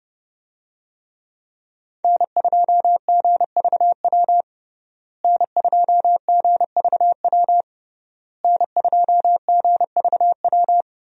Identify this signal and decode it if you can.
Morse code, heard as a keyed tone, about 30 wpm, 700 Hz. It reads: N2GVW N2GVW N2GVW
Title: USA Call Signs - 5 Characters - Repeated 3x - 30wpm